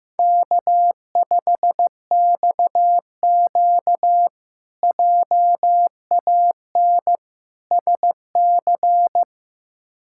Morse code, heard as a keyed tone, about 15 wpm, 700 Hz. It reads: K5XQ JAN SC
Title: K1USN SST QSO Practice - 15wpm